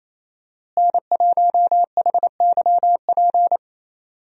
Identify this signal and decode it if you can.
Morse code, heard as a keyed tone, about 27 wpm, 700 Hz. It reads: N1HYP
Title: USA Call Signs - 5 Characters - Repeated 2x - 28wpm